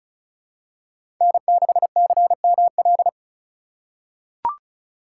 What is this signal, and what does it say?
Morse code, heard as a keyed tone, about 36 wpm, 700 Hz. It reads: N6CML
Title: USA Call Signs - 5 Characters - 2x Character Spacing - 35wpm